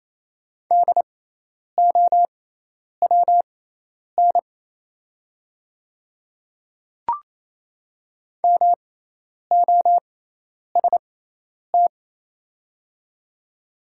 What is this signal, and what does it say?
Morse code, heard as a keyed tone, about 28 wpm, spaced as for 10 wpm, 700 Hz. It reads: DOWN MOST
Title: 6x Character Spacing - Top 100 Words - 28wpm